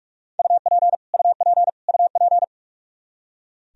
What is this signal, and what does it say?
Morse code, heard as a keyed tone, about 46 wpm, 700 Hz. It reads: UP UP UP
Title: Top 100 Words Repeated 45wpm